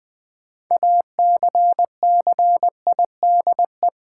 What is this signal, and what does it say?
Morse code, heard as a keyed tone, about 20 wpm, 700 Hz. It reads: ACCIDE
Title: Build A Word - 6 Letters At A Time - 12-Letter Words - 20wpm